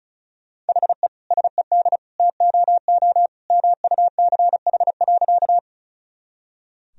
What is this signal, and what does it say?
Morse code, heard as a keyed tone, about 34 wpm, 700 Hz. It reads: HE SED TOO MUCH.
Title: Sentences from Top 200 Words and Abbreviations 35wpm